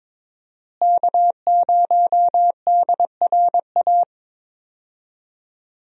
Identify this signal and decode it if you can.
Morse code, heard as a keyed tone, about 21 wpm, 700 Hz. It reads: K0DRA
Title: USA Call Signs - 5 Characters - Repeated 2x - 22wpm